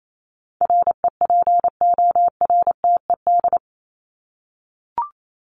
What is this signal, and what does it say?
Morse code, heard as a keyed tone, about 28 wpm, 700 Hz. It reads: REPORTED
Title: ED Suffix - Top 100 Words with Base Words 28wpm